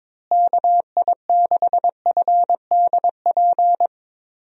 Morse code, heard as a keyed tone, about 22 wpm, 700 Hz. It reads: KI6FDP
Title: USA Call Signs - 6 Characters - Repeated 3x - 22wpm